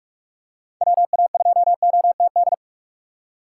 Morse code, heard as a keyed tone, about 42 wpm, 700 Hz. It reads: WA2OTD
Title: USA Call Signs - 6 Characters - 6x Character Spacing - 45wpm